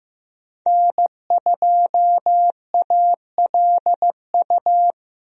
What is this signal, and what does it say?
Morse code, heard as a keyed tone, about 15 wpm, 700 Hz. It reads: N2ALU
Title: USA Call Signs - 5 Characters - 15wpm